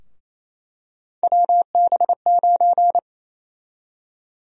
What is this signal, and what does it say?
Morse code, heard as a keyed tone, about 28 wpm, 700 Hz. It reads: WB9
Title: USA Call Sign Prefixes - 28wpm